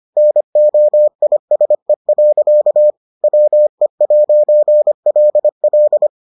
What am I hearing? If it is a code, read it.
Morse code, heard as a keyed tone, about 25 wpm, 600 Hz. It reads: NOISE. WE'LL